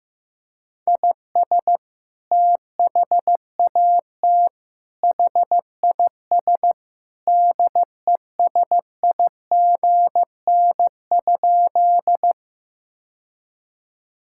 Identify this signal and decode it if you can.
Morse code, heard as a keyed tone, about 15 wpm, 700 Hz. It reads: IS THAT HIS DESIGN?
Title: Sentences from Top 1000 Words - Review Part 5 - 15wpm